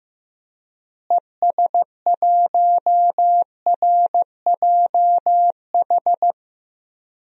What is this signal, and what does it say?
Morse code, heard as a keyed tone, about 15 wpm, 700 Hz. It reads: ES1RJH
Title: DXCC-100 Call Signs - 6 Characters - Repeated 2x - 15wpm